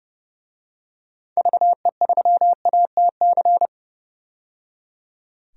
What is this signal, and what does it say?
Morse code, heard as a keyed tone, about 30 wpm, 700 Hz. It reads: VE3ATC
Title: DXCC-100 Call Signs - 6 Characters - 30wpm